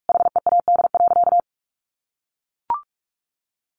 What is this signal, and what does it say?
Morse code, heard as a keyed tone, about 44 wpm, 700 Hz. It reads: HEAD.
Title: Sentences from Top 1400 Words 45wpm